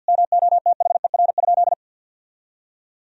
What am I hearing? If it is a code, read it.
Morse code, heard as a keyed tone, about 53 wpm, 700 Hz. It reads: MOTHER?
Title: Sentences from Top 1000 Words - Review Part 2 - 50wpm